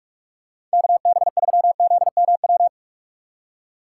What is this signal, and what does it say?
Morse code, heard as a keyed tone, about 46 wpm, 700 Hz. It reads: KB3ZMW
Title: USA Call Signs - 6 Characters - 4x Character Spacing - 45wpm